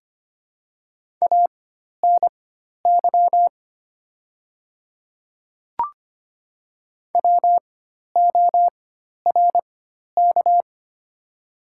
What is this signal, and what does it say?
Morse code, heard as a keyed tone, about 25 wpm, spaced as for 12 wpm, 700 Hz. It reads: ANY WORK
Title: 4x Character Spacing - Top 300 Words - 25wpm